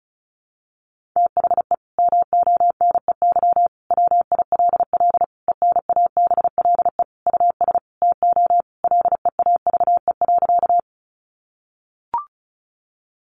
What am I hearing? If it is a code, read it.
Morse code, heard as a keyed tone, about 34 wpm, 700 Hz. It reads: THE MONEY WILL ENABLE US TO LEAVE.
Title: Sentences from Top 1500 Words 35wpm